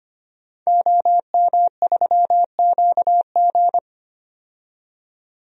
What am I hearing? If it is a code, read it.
Morse code, heard as a keyed tone, about 25 wpm, 700 Hz. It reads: OM3QG
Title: DXCC-100 Call Signs - 5 Characters - 25wpm